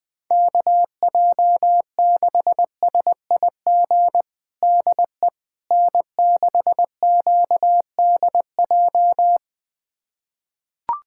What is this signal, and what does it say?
Morse code, heard as a keyed tone, about 20 wpm, 700 Hz. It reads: KJ6SIG DE N6QDJ
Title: USA Callsign DE Callsign 20wpm